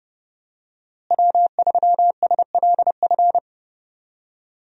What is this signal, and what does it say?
Morse code, heard as a keyed tone, about 30 wpm, 700 Hz. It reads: W3SLF
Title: USA Call Signs - 5 Characters - Repeated 3x - 30wpm